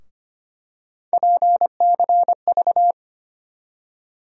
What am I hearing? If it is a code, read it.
Morse code, heard as a keyed tone, about 25 wpm, 700 Hz. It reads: PCV
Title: USA Call Sign Postfixes - 3 Letters 25wpm